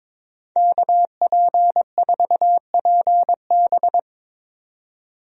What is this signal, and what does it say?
Morse code, heard as a keyed tone, about 21 wpm, 700 Hz. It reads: KP4PB
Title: DXCC-100 Call Signs - 5 Characters - 22wpm